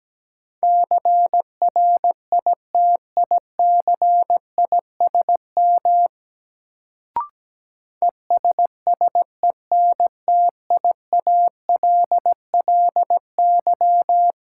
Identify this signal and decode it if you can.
Morse code, heard as a keyed tone, about 17 wpm, 700 Hz. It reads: CRITICISM ESSENTIALLY
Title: Sentences from Top 1300 Words - New Word Drill Alt - 17wpm